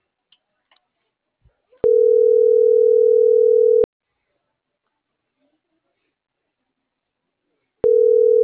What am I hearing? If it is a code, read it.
Ringback tone (North America), 440 + 480 Hz.